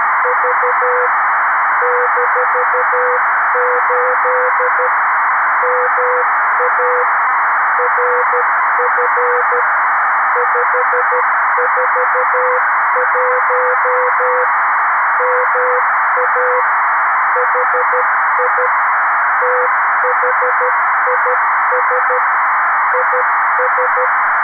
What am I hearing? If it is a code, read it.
Morse code, heard as a keyed tone, about 13 wpm, 500 Hz, with noise. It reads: V -8 MA RF 541 MA HI THIS IS
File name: arissat / recordings / ARISSat 11Sept2011 1354utc 60frames